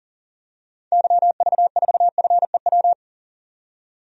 Morse code, heard as a keyed tone, about 40 wpm, 700 Hz. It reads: YV4FEW
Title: DXCC-100 Call Signs - 6 Characters - 40wpm